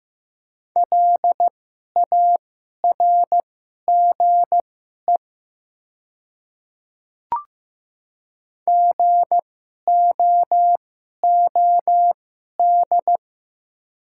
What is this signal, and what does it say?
Morse code, heard as a keyed tone, about 15 wpm, spaced as for 11 wpm, 700 Hz. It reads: LARGE GOOD
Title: 2x Character Spacing - Top 300 Words - 15wpm